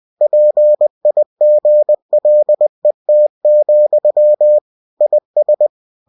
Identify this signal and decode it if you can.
Morse code, heard as a keyed tone, about 20 wpm, 600 Hz. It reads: PIGLET, IS
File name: cw-books / winnie-the-pooh / winnie-20wpm-page-0082